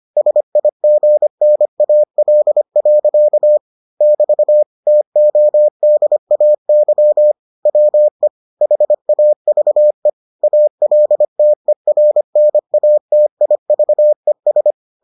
Morse code, heard as a keyed tone, about 25 wpm, 600 Hz. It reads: SIGNAL. <BT> TODAY WE HAVE ALTERNATIVES